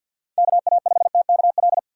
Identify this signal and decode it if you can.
Morse code, heard as a keyed tone, about 47 wpm, 700 Hz. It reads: KA5TKL
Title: USA Call Signs - 6 Characters - 4x Character Spacing - 50wpm